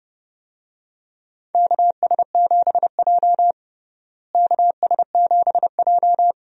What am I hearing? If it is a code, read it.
Morse code, heard as a keyed tone, about 29 wpm, 700 Hz. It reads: KS7J KS7J
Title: USA Call Signs - 4 Characters - Repeated 2x - 30wpm